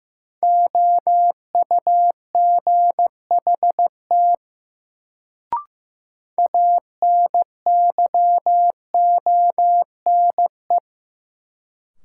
Morse code, heard as a keyed tone, about 15 wpm, 700 Hz. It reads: OUGHT ANYONE
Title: Sets of 5 Words Encore - New Word Drill - 15wpm